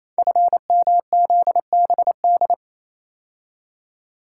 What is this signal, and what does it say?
Morse code, heard as a keyed tone, about 27 wpm, 700 Hz. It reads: FMZBD
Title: Code Groups - 5 Letters - 28wpm